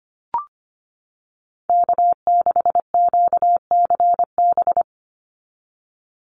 Morse code, heard as a keyed tone, about 25 wpm, 700 Hz. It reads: K6QCB
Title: NA SOTA Chaser Callsigns 25wpm